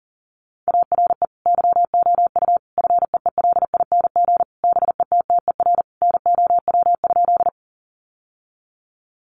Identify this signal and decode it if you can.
Morse code, heard as a keyed tone, about 37 wpm, 700 Hz. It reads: ARE YOU FEELING BETTER NOW?
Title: Sentences from Top 1200 Words 40wpm